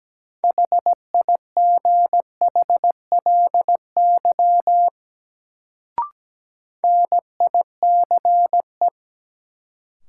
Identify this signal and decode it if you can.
Morse code, heard as a keyed tone, about 17 wpm, 700 Hz. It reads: HIGHLY NICE
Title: Sentences from Top 1100 Words - New Word Drill Alt - 17wpm